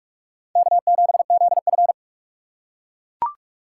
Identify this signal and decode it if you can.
Morse code, heard as a keyed tone, about 38 wpm, 700 Hz. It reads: K7ZF